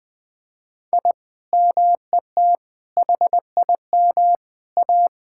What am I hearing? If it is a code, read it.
Morse code, heard as a keyed tone, about 20 wpm, 700 Hz. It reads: I MET HIM A
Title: Sentences with 5-Word Phrases - 20wpm